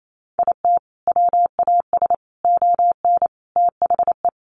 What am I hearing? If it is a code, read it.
Morse code, heard as a keyed tone, about 28 wpm, 700 Hz. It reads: IT WAS ON THE
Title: Sentences from Top 1200 Words 28wpm